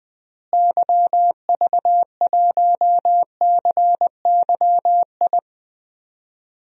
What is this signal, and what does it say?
Morse code, heard as a keyed tone, about 20 wpm, 700 Hz. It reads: YV1CYI
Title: DXCC-100 Call Signs - 6 Characters - Repeated 2x - 20wpm